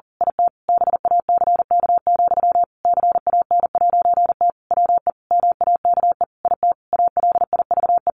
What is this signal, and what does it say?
Morse code, heard as a keyed tone, about 40 wpm, 700 Hz. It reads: IT BACK, CAN'T WE MAKE IT ALIVE
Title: A Sound Of Thunder 40wpm